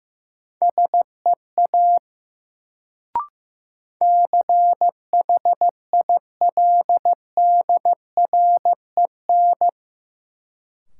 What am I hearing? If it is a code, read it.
Morse code, heard as a keyed tone, about 15 wpm, 700 Hz. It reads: SEA CHILDREN
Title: Sentences from Top 300 Words - New Word Drill - 15wpm